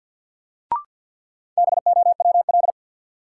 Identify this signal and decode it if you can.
Morse code, heard as a keyed tone, about 52 wpm, 700 Hz. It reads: BOWL
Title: Sentences from Top 1200 Words - New Word Drill Alt - 50wpm